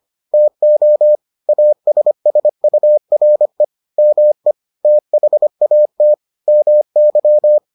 Morse code, heard as a keyed tone, about 25 wpm, 600 Hz. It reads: TO ASSURE ME THAT MY